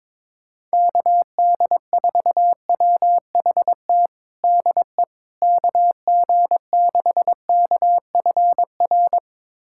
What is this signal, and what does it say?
Morse code, heard as a keyed tone, about 22 wpm, 700 Hz. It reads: KD4WHT DE KG6KFR
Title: USA Callsign DE Callsign - 6 Characters - 22wpm